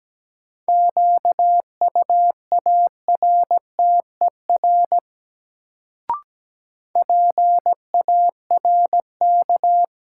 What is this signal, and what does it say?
Morse code, heard as a keyed tone, about 17 wpm, 700 Hz. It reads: QUARTER PARK
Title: Sentences from Top 1300 Words - New Word Drill Alt - 17wpm